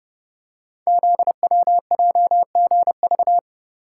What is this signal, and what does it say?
Morse code, heard as a keyed tone, about 30 wpm, 700 Hz. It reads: ZWJGV
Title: Code Groups - 5 Letters - 30wpm